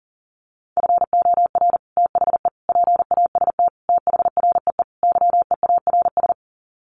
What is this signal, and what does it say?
Morse code, heard as a keyed tone, about 40 wpm, 700 Hz. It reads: FOR THE PAST THREE YEARS
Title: Sets of 5 Words Encore2 40wpm